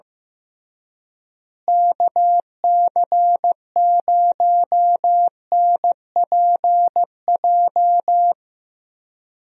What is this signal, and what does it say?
Morse code, heard as a keyed tone, about 15 wpm, 700 Hz. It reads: KC0NPJ
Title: USA Call Signs - 6 Characters - Repeated 3x - 15wpm